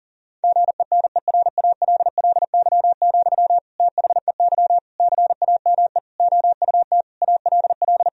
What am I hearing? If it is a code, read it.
Morse code, heard as a keyed tone, about 40 wpm, 700 Hz. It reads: GENERALLY, THEY CAME OUT ALL
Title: Sentences from Top 1000 Words - Review Part 4 - 40wpm